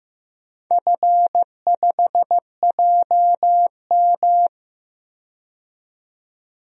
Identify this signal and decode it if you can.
Morse code, heard as a keyed tone, about 15 wpm, 700 Hz. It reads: F5JM
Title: DXCC-100 Call Signs - 4 Characters - 15wpm